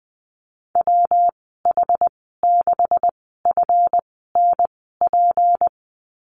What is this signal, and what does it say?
Morse code, heard as a keyed tone, about 20 wpm, spaced as for 15 wpm, 700 Hz. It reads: WH6FNP
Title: 186 Lesson 30g - Letter b - Call Signs 20wpm